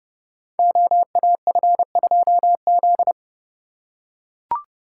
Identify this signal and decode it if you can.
Morse code, heard as a keyed tone, about 30 wpm, 700 Hz. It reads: OAF2Z
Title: Code Groups - 5 Letter:Numbers - 30wpm